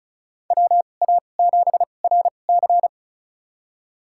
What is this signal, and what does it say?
Morse code, heard as a keyed tone, about 36 wpm, spaced as for 25 wpm, 700 Hz. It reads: WA7RC
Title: USA Call Signs - 2x Character Spacing - 35wpm